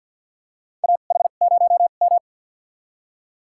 Morse code, heard as a keyed tone, about 47 wpm, spaced as for 35 wpm, 700 Hz. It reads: AH0M
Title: USA Call Signs - 4 Characters - 2x Character Spacing - 50wpm